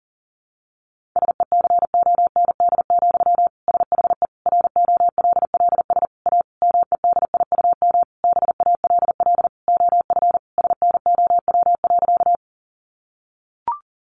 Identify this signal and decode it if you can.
Morse code, heard as a keyed tone, about 40 wpm, 700 Hz. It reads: SECOND, SHE ROLLS A MEDIUM BALL OF SNOW.
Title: Easy Phrases 40wpm